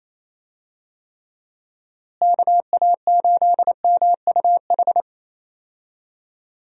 Morse code, heard as a keyed tone, about 28 wpm, 700 Hz. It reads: KA8MUH
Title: USA Call Signs - 6 Characters - Repeated 3x - 28wpm